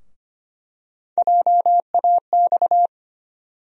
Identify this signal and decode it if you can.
Morse code, heard as a keyed tone, about 25 wpm, 700 Hz. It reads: JAX
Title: USA Call Sign Postfixes - 3 Letters 25wpm